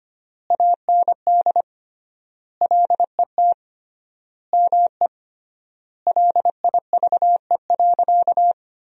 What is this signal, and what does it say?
Morse code, heard as a keyed tone, about 25 wpm, 700 Hz. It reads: AND LET ME LIVE.
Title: 3x Word-Spacing - Sentences from Top 300 Words - 25wpm